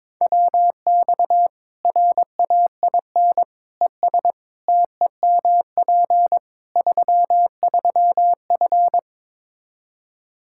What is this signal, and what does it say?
Morse code, heard as a keyed tone, about 22 wpm, 700 Hz. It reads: WX RAIN ES TEMP 33F
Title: Practice QSO - Exchange 2 - Info - 22wpm